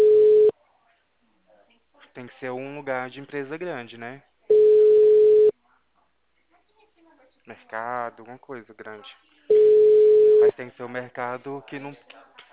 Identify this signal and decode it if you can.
Ringback tone (Europe, 425 Hz), with speech.